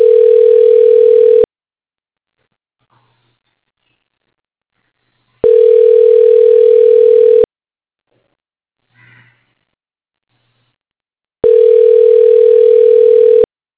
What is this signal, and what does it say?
Ringback tone (North America), 440 + 480 Hz.